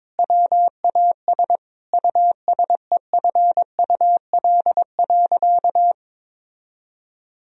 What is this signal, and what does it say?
Morse code, heard as a keyed tone, about 22 wpm, 700 Hz. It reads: WAS USEFUL.
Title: Sentences from Top 1000 Words - Review Part 4 - 22wpm